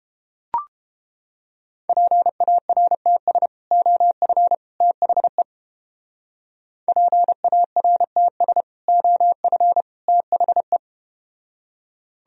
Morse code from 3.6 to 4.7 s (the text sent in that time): OF